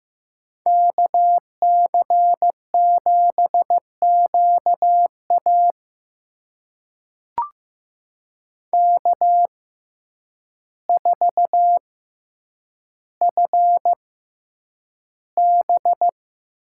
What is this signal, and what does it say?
Morse code, heard as a keyed tone, about 15 wpm, spaced as for 8 wpm, 700 Hz. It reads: KC7QA K4FB
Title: USA Call Signs - 5 Characters - 6x Character Spacing - 15wpm